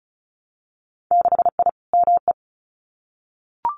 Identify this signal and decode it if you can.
Morse code, heard as a keyed tone, about 36 wpm, 700 Hz. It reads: 6I ME
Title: Winter Field Day Exchanges Encore 35wpm